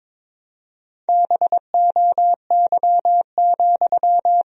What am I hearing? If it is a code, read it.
Morse code, heard as a keyed tone, about 22 wpm, 700 Hz. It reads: BOY,
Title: Sentences from Top 1000 Words - Review Part 2 - 22wpm